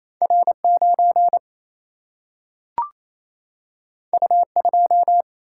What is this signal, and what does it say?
Morse code, heard as a keyed tone, about 28 wpm, 700 Hz. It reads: R 9 U 2
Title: Single-Letter Single-Number 28wpm